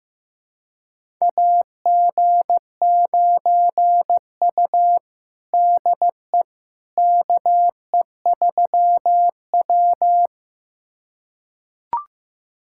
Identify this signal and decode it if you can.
Morse code, heard as a keyed tone, about 15 wpm, 700 Hz. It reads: AG9U DE KE3W
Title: USA Callsign DE Callsign - 4 Characters - 15wpm